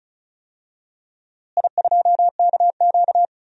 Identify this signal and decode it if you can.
Morse code, heard as a keyed tone, about 34 wpm, 700 Hz. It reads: I2KQ
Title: DXCC-100 Call Signs - 4 Characters - Repeated 3x - 35wpm